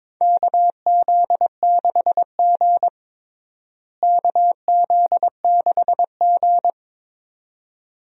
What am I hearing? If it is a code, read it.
Morse code, heard as a keyed tone, about 22 wpm, 700 Hz. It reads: KZ6G KZ6G
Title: USA Call Signs - 4 Characters - Repeated 2x - 22wpm